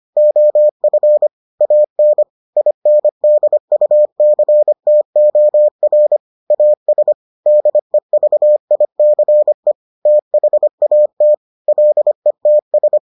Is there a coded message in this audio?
Yes: Morse code from 0.2 to 13.0 s, heard as a keyed tone, about 25 wpm, 600 Hz. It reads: OF AN INDUCTOR AS DEVICE THAT LETS